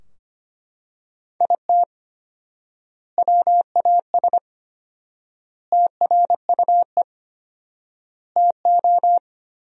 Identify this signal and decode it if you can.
Morse code, heard as a keyed tone, about 25 wpm, 700 Hz. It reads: IT WAS TRUE TO
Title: 4x Word-Spacing - Sentences from Top 400 Words - 25wpm